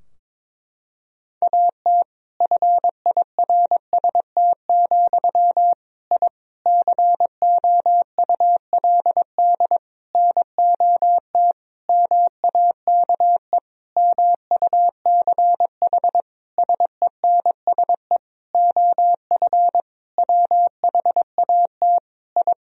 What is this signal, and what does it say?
Morse code, heard as a keyed tone, about 21 wpm, 700 Hz. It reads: AT FIRST, I COULD NOT MAKE MUCH SENSE OF WHAT I
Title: Sentences from Top 1000 Words - Review Part 4 - 22wpm